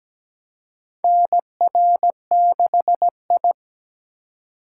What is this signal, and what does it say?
Morse code, heard as a keyed tone, about 17 wpm, 700 Hz. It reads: NR6I
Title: USA Call Signs - 4 Characters - 2x Character Spacing - 17wpm